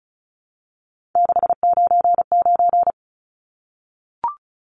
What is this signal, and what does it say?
Morse code, heard as a keyed tone, about 34 wpm, 700 Hz. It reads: 699
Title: Numbers - 3-Digit 35wpm